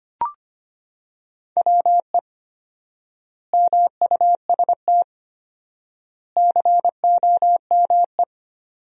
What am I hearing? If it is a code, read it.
Morse code, heard as a keyed tone, about 25 wpm, 700 Hz. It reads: WE MUST COME
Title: 4x Word-Spacing - Sentences from Top 200 Words - 25wpm